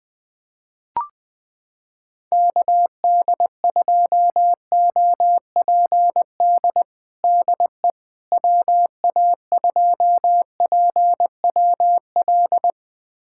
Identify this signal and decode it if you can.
Morse code, heard as a keyed tone, about 21 wpm, 700 Hz. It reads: KD2OPD DE WA2PWL
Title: USA Callsign DE Callsign 20wpm